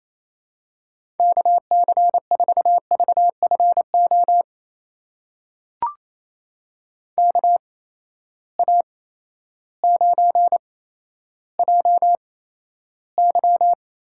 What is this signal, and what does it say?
Morse code, heard as a keyed tone, about 28 wpm, spaced as for 13 wpm, 700 Hz. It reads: KC4VFO KA9JY
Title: USA Call Signs - 6 Characters - 8x Character Spacing - 28wpm